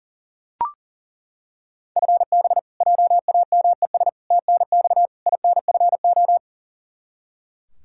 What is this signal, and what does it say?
Morse code, heard as a keyed tone, about 40 wpm, 700 Hz. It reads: FB JAMES TNX INFO
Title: Practice QSO - Exchange 3 - More Info - 40wpm